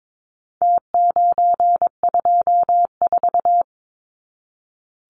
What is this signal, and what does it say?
Morse code, heard as a keyed tone, about 22 wpm, 700 Hz. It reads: T924
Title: Numbers - Contest Style - 4-Digit 22wpm